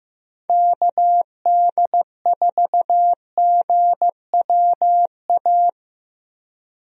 Morse code, heard as a keyed tone, about 15 wpm, 700 Hz. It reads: KD4GWA
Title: USA Call Signs - 6 Characters - Repeated 2x - 15wpm